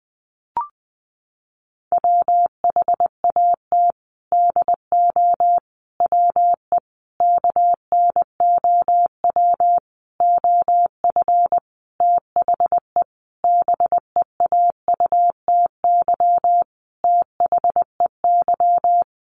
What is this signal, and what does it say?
Morse code, heard as a keyed tone, about 20 wpm, 700 Hz. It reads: WHAT DO WE KNOW OF THE BEAUTY THEY
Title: Sentences from Top 1400 Words 20wpm